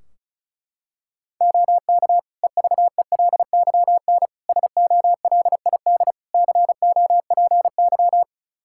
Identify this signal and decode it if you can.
Morse code, heard as a keyed tone, about 36 wpm, 700 Hz. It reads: OK EVELYN SOLID COPY
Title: Practice QSO - Exchange 3 - More Info - 35wpm